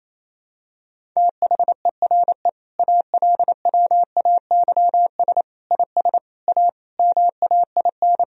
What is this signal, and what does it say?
Morse code, heard as a keyed tone, about 27 wpm, 700 Hz. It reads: THERE ALWAYS IS A MAIN
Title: Sentences from Top 1000 Words - Review Part 2 - 28wpm